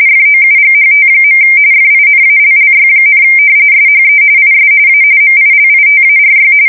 RTTY, about 45.45 baud. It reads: FOUGHT. FINALLY, SHE IS A MILITARY
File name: fsk170 / battleshipnj